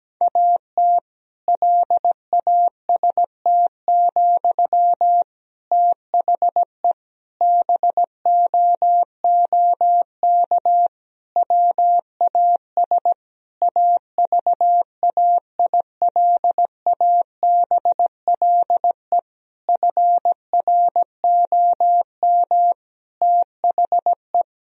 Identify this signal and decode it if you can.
Morse code, heard as a keyed tone, about 17 wpm, 700 Hz. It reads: AT LAST, THE BOOK WAS AVAILABLE FROM THE
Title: Sentences from Top 1000 Words - Review Part 3 - 17wpm